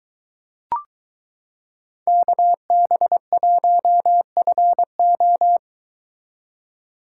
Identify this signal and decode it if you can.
Morse code, heard as a keyed tone, about 22 wpm, 700 Hz. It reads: KB1FO